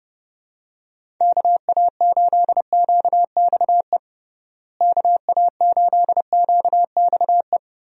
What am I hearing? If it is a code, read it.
Morse code, heard as a keyed tone, about 30 wpm, 700 Hz. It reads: KA8QXE KA8QXE
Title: USA Call Signs - 6 Characters - Repeated 3x - 30wpm